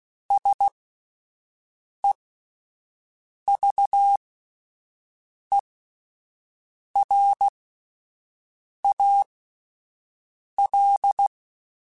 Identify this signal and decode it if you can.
Morse code, heard as a keyed tone, about 16 wpm, spaced as for 5 wpm, 800 Hz. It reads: SEVERAL
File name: QSOs Spaced